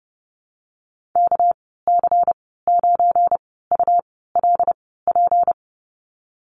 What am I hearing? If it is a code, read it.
Morse code, heard as a keyed tone, about 28 wpm, spaced as for 17 wpm, 700 Hz. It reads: KC9ULP
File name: Lesson 37g - Letter x - Call Signs 30wpm